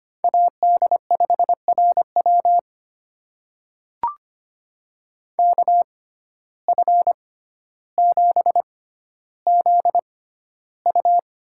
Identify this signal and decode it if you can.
Morse code, heard as a keyed tone, about 25 wpm, spaced as for 13 wpm, 700 Hz. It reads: AD5RW KF7ZU
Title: USA Call Signs - 5 Characters - 6x Character Spacing - 25wpm